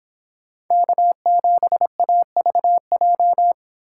Morse code, heard as a keyed tone, about 26 wpm, 700 Hz. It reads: K7AVJ